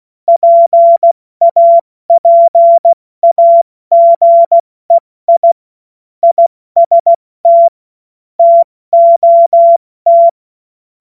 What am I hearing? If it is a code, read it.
Morse code, heard as a keyed tone, about 16 wpm, spaced as for 14 wpm, 680 Hz. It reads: PAPAGEI IST TOT
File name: Morsekurs / Witze / Witz 28. September 2021, 70 F80